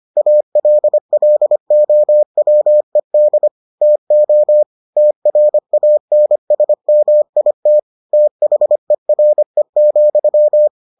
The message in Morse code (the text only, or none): ALLOWED TO TRANSMIT THERE,